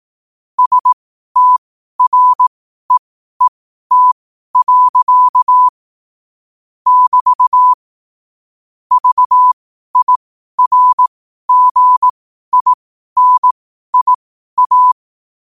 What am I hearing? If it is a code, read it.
Morse code, heard as a keyed tone, about 18 wpm, spaced as for 12 wpm, 1000 Hz. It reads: STREET. <BT> VIRGINIA
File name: qotdfiles / 12wpm